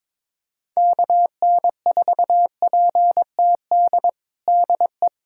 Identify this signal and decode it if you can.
Morse code, heard as a keyed tone, about 22 wpm, 700 Hz. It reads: KN4PTD DE
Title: USA Callsign DE Callsign - 6 Characters - 22wpm